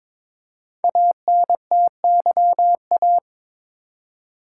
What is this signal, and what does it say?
Morse code, heard as a keyed tone, about 22 wpm, 700 Hz. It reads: ANTYA
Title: Code Groups - 5 Letter:Numbers - 22wpm